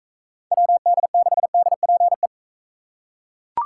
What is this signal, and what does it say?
Morse code, heard as a keyed tone, about 41 wpm, 700 Hz. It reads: WD6DPE